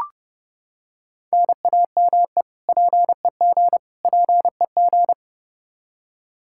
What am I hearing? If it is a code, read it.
Morse code, heard as a keyed tone, about 30 wpm, 700 Hz. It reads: NAME PEG PEG
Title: Practice QSO - Exchange 1 - Beginning - 30wpm